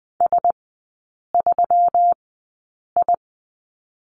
Morse code, heard as a keyed tone, about 20 wpm, spaced as for 8 wpm, 700 Hz. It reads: S3I
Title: 172 Lesson 28e - Number 9 - Three Characters 20wpm